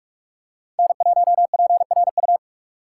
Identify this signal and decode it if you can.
Morse code, heard as a keyed tone, about 43 wpm, 700 Hz. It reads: N1PRU